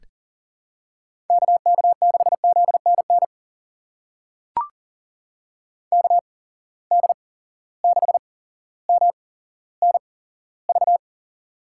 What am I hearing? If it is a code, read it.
Morse code, heard as a keyed tone, about 40 wpm, spaced as for 17 wpm, 700 Hz. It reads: KK6ZNN KD6MNV
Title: USA Call Signs - 6 Characters - 8x Character Spacing - 40wpm